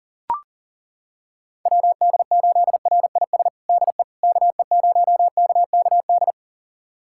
Morse code, heard as a keyed tone, about 37 wpm, 700 Hz. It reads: WD8RIS DE KE0KKD
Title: USA Callsign DE Callsign - 6 Characters - 40wpm